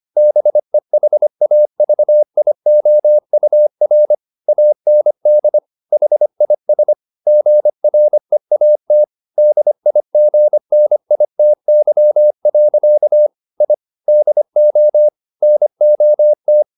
Morse code, heard as a keyed tone, about 25 wpm, 600 Hz. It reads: BEHAVIOUR AND HIS GREAT DIGNITY. I DO NOT